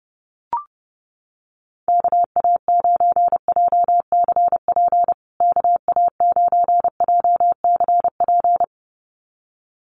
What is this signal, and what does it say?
Morse code, heard as a keyed tone, about 30 wpm, 700 Hz. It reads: KA9JCP KA9JCP
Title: USA Call Signs Repeated 2x 30wpm